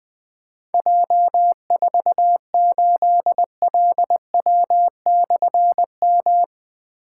Morse code, heard as a keyed tone, about 20 wpm, 700 Hz. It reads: J48LW/M
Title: Difficult Call Signs 20wpm